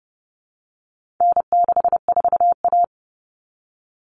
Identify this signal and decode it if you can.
Morse code, heard as a keyed tone, about 30 wpm, 700 Hz. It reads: N64A
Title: Common Cut Numbers - 4-Digit 30wpm